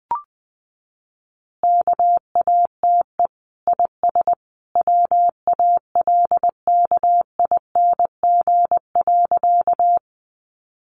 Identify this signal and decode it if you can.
Morse code, heard as a keyed tone, about 20 wpm, 700 Hz. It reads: KATE IS WALKING.
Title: Easy Phrases 20wpm